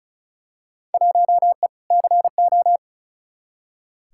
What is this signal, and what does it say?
Morse code, heard as a keyed tone, about 34 wpm, 700 Hz. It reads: 1E CO
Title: Field Day Exchanges - 35wpm